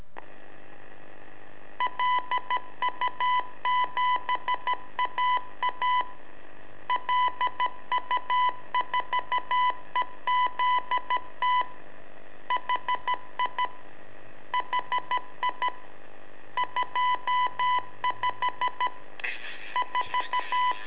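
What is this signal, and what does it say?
Morse code, heard as a keyed tone, about 13 wpm, 1000 Hz. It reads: LU7AA LU4EZT HI HI 25 4
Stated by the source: Telemetry in Morse from LU7AA and LU4EZ, AMSAT and Turma Mar del Plata, sent from a balloon.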